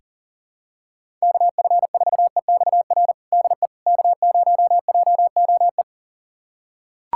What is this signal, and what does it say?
Morse code, heard as a keyed tone, about 40 wpm, 700 Hz. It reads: KF4EXR DE K0JOE
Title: USA Callsign DE Callsign 40wpm